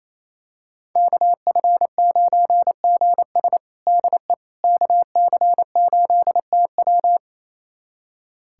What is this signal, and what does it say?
Morse code, heard as a keyed tone, about 28 wpm, 700 Hz. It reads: KF9GS DE KC8TW
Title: USA Callsign DE Callsign - 5 Characters - 28wpm